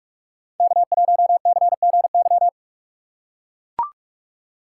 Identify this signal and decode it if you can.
Morse code, heard as a keyed tone, about 44 wpm, 700 Hz. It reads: K1CGY